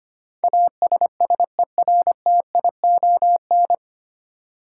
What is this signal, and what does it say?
Morse code, heard as a keyed tone, about 25 wpm, 700 Hz. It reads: ASSERTION
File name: practice / 25wpm / 9 Letter Words - First 3 Letters Given - 25wpm